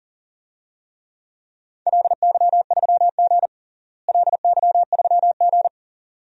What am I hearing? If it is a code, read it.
Morse code, heard as a keyed tone, about 38 wpm, 700 Hz. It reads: LY3G LY3G
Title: DXCC-100 Call Signs - 4 Characters - Repeated 2x - 40wpm